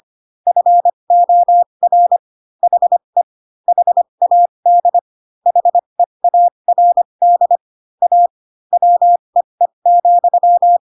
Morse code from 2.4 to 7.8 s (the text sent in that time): HE HAD HEARD